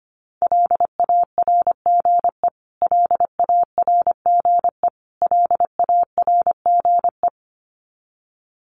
Morse code, heard as a keyed tone, about 25 wpm, 700 Hz. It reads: LARGE LARGE LARGE
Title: Top 200 Words Repeated 25wpm